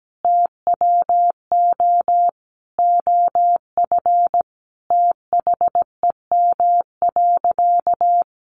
Morse code, heard as a keyed tone, about 17 wpm, 700 Hz. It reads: TWO OF THEM.
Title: Sentences from Top 100 Words 17wpm